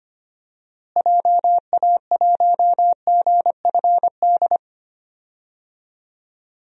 Morse code, heard as a keyed tone, about 25 wpm, 700 Hz. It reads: JA1GFD
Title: DXCC-100 Call Signs - 6 Characters - Repeated 3x - 25wpm